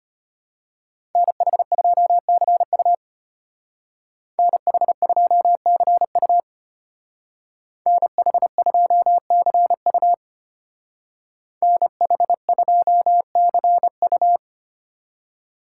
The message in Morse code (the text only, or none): NH2CU NH2CU NH2CU NH2CU